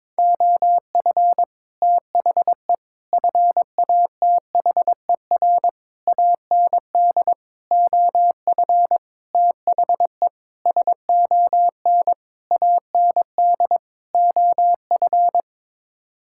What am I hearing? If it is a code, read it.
Morse code, heard as a keyed tone, about 22 wpm, 700 Hz. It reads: OF THE FATHER AND OF THE SON AND OF
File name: Sets of 9 Words - 22wpm